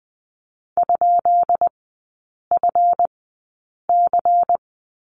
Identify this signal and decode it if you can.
Morse code, heard as a keyed tone, about 20 wpm, spaced as for 8 wpm, 690 Hz. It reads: ?FC
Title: 145 Lesson 24e - Letter y - Three Characters 20wpm